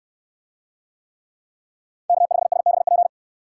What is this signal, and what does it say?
Morse code, heard as a keyed tone, about 69 wpm, 700 Hz. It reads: K5SBP